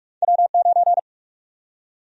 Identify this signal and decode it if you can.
Morse code, heard as a keyed tone, about 42 wpm, 700 Hz. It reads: W9
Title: USA Call Sign Prefixes - Repeated 2x - 45wpm